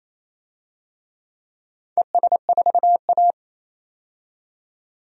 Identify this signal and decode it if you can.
Morse code, heard as a keyed tone, about 28 wpm, 700 Hz. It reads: ES4A
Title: DXCC-100 Call Signs - 4 Characters - 28wpm